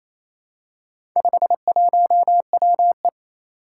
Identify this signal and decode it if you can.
Morse code, heard as a keyed tone, about 29 wpm, 700 Hz. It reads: 51WE
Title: Four Challenging Letter-Number Combinations 28wpm